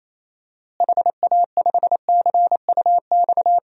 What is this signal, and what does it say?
Morse code, heard as a keyed tone, about 27 wpm, 700 Hz. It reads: HA5CUX
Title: DXCC-100 Call Signs - 6 Characters - Repeated 2x - 28wpm